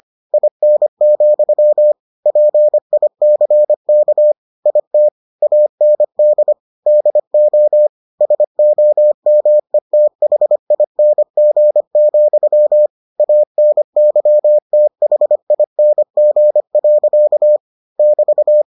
Morse code, heard as a keyed tone, about 25 wpm, 600 Hz. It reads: IN, PICK IT AND DO SOMETHING, ANYTHING. <BT>